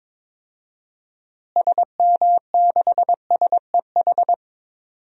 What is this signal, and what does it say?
Morse code, heard as a keyed tone, about 22 wpm, 700 Hz. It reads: SM6SEH
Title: DXCC-100 Call Signs - 6 Characters - Repeated 2x - 22wpm